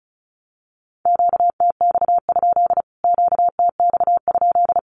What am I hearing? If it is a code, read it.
Morse code, heard as a keyed tone, about 34 wpm, 700 Hz. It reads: QTX? QTX?
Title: Q-Codes Repeated 3x 35wpm